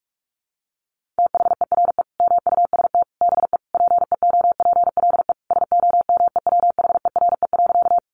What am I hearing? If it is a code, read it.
Morse code, heard as a keyed tone, about 44 wpm, 700 Hz. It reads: THERE MUST BE PEOPLE SOMEWHERE.
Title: Sentences from Top 1400 Words 45wpm